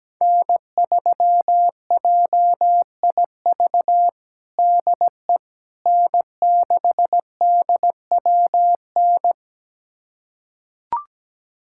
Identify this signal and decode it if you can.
Morse code, heard as a keyed tone, about 17 wpm, 700 Hz. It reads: N3JIV DE N6DWN
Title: USA Callsign DE Callsign - 5 Characters - 17wpm